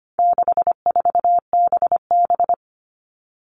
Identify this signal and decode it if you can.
Morse code, heard as a keyed tone, about 25 wpm, 700 Hz. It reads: 64BB
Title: All Cut Numbers - 4-Digit 25wpm